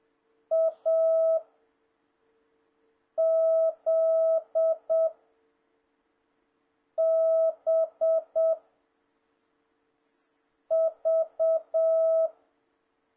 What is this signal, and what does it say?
Morse code, heard as a keyed tone, about 7 wpm, 650 Hz. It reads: A Z B V
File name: jh8cbh / 2026cw